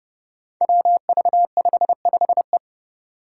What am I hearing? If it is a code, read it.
Morse code, heard as a keyed tone, about 31 wpm, 700 Hz. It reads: WV55E
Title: Five Challenging Letter-Number Combinations 30wpm